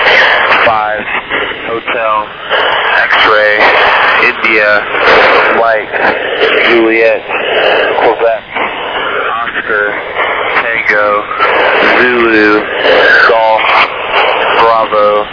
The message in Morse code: none

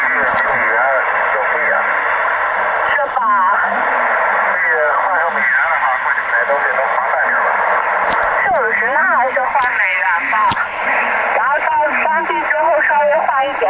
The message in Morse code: none

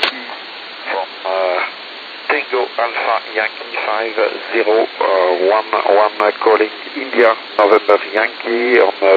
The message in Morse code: none

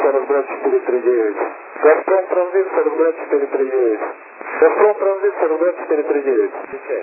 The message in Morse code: none

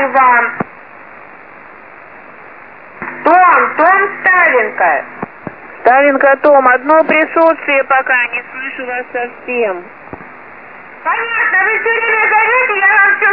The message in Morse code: none

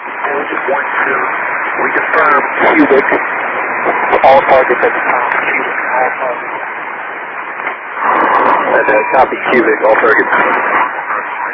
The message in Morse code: none

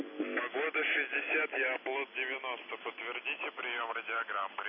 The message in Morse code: none

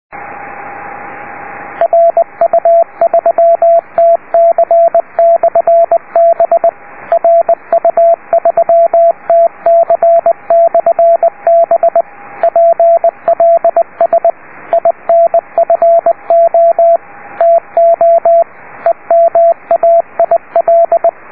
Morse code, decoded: RU3TC/B RU3TC/B PLS INFO TO EMAIL

